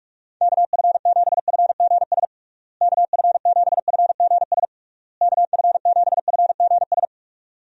Morse code, heard as a keyed tone, about 44 wpm, 700 Hz. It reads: KF7FGS KF7FGS KF7FGS
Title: USA Call Signs - 6 Characters - Repeated 3x - 45wpm